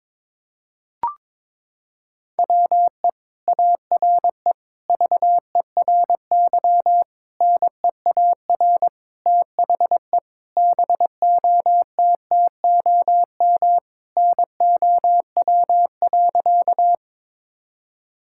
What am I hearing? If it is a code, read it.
Morse code, heard as a keyed tone, about 22 wpm, 700 Hz. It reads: WE ARE VERY NEAR THE BOTTOM NOW.
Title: Sentences from Top 1000 Words - Review Part 6 - 22wpm